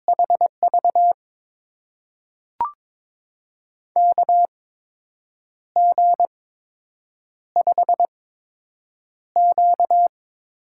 Morse code, heard as a keyed tone, about 22 wpm, spaced as for 7 wpm, 700 Hz. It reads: HV KG5Q
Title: USA Call Signs - 6 Characters - 8x Character Spacing - 22wpm